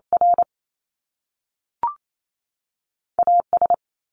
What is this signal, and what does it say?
Morse code, heard as a keyed tone, about 28 wpm, 700 Hz. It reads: R A S
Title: Top Two Letter Combinations in Words 28wpm